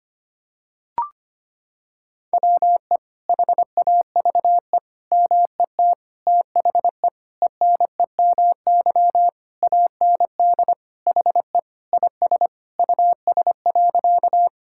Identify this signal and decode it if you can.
Morse code, heard as a keyed tone, about 25 wpm, 700 Hz. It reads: WE HAVE MET THE ENEMY AND HE IS US.
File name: practice / 25wpm / Sentences with 9-Word Phrases - 25wpm